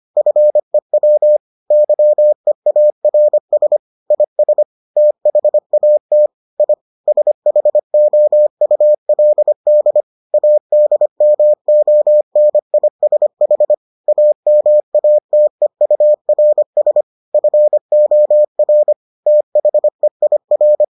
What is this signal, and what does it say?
Morse code, heard as a keyed tone, about 25 wpm, 600 Hz. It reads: FEW YEARS IS THAT I SHOULD ADMONISH AMATEURS FOR THEIR